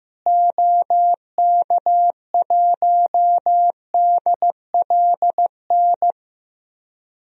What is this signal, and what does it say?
Morse code, heard as a keyed tone, about 15 wpm, 700 Hz. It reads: OK1DLN
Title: DXCC-100 Call Signs - 6 Characters - Repeated 3x - 15wpm